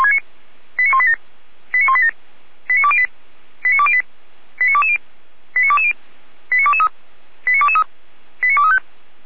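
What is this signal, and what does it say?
Five-tone selective call (ZVEI), heard as quick runs of five tones, tones of about 70 ms. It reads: 89187 89187 89208 89208 89220 89220 89223 89223 89235
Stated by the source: ZVEI five-tone selective call received via SDR by Feuerwehr Orlamünde.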